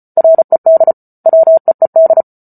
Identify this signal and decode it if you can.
Morse code, heard as a keyed tone, about 34 wpm, 650 Hz. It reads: RED WEED